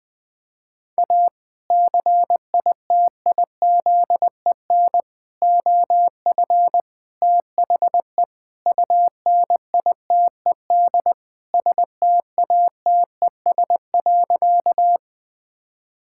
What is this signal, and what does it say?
Morse code, heard as a keyed tone, about 20 wpm, 700 Hz. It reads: A CITIZEN OF THE UNITED STATES.
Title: Sentences with 3-Word Phrases - Spoken Beginning - 20wpm